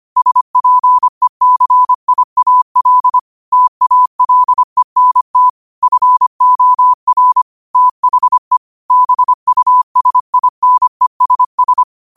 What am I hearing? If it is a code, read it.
Morse code, heard as a keyed tone, about 25 wpm, 1000 Hz. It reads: SPECIAL TALENT FOR THE BUSINESS
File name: qotdfiles / 25wpm